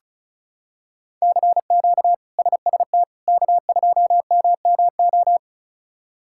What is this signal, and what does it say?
Morse code, heard as a keyed tone, about 34 wpm, 700 Hz. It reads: CQ SST K2MMO
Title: K1USN SST QSO Practice - 35wpm